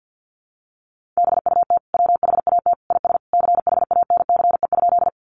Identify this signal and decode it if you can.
Morse code, heard as a keyed tone, about 50 wpm, 700 Hz. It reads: BUT WHAT IS CHANCE?
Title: Sentences from Top 800 Words 50wpm